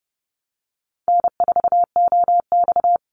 Morse code, heard as a keyed tone, about 30 wpm, 700 Hz. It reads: N4OX
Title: Easy Call Signs Repeated 3x - 30wpm